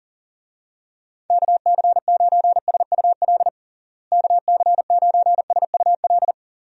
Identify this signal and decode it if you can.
Morse code, heard as a keyed tone, about 40 wpm, 700 Hz. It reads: KC9SUL KC9SUL
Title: USA Call Signs - 6 Characters - Repeated 2x - 40wpm